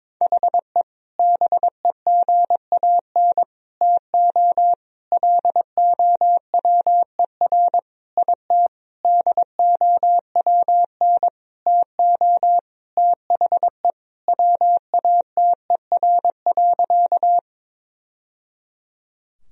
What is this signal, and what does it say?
Morse code, heard as a keyed tone, about 22 wpm, 700 Hz. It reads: HE BEGAN TO LOWER IT DOWN TO THE WATER.
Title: Sentences from Top 1000 Words - Review Part 4 - 22wpm